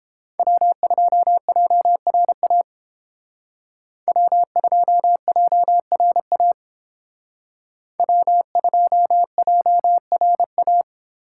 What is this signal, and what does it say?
Morse code, heard as a keyed tone, about 30 wpm, 700 Hz. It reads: W2JRA W2JRA W2JRA